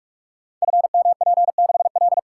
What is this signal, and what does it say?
Morse code, heard as a keyed tone, about 44 wpm, 700 Hz. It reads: FMP6L
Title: Code Groups - 5 Letter:Numbers - 45wpm